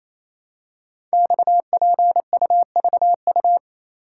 Morse code, heard as a keyed tone, about 28 wpm, 700 Hz. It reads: XPUVU
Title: Code Groups - 5 Letters - 28wpm